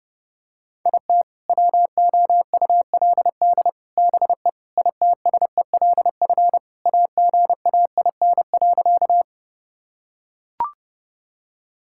Morse code, heard as a keyed tone, about 32 wpm, 700 Hz. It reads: IT WOULD BE ITSELF AGAIN.
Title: Sentences from Top 1000 Words - Review Part 5 - 30wpm